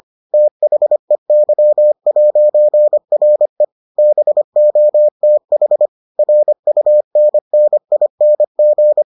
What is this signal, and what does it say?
Morse code, heard as a keyed tone, about 25 wpm, 600 Hz. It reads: THEY'RE BOTH RUNNING